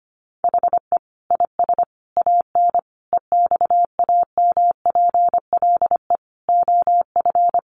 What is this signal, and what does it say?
Morse code, heard as a keyed tone, about 25 wpm, 700 Hz. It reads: HE IS AN EXAMPLE OF
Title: Sentences from Top 1500 Words 25wpm